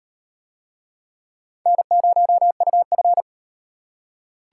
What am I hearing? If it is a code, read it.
Morse code, heard as a keyed tone, about 35 wpm, 700 Hz. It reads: N0UF